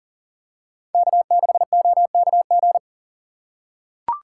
Morse code, heard as a keyed tone, about 40 wpm, 700 Hz. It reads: K6OKG
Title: USA Call Signs - 5 Characters - 3x Character Spacing - 40wpm